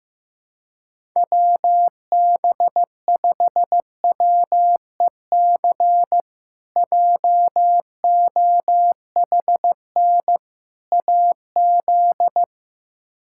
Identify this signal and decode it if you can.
Morse code, heard as a keyed tone, about 16 wpm, 700 Hz. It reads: WB5WEC JOHN AZ
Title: K1USN SST QSO Practice - 15wpm